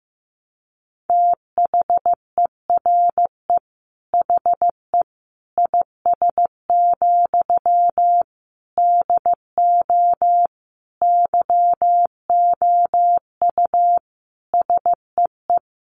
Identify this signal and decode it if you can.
Morse code, heard as a keyed tone, about 15 wpm, 700 Hz. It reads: THERE HE IS, DO YOU SEE
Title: Sentences from Top 100 Words 15wpm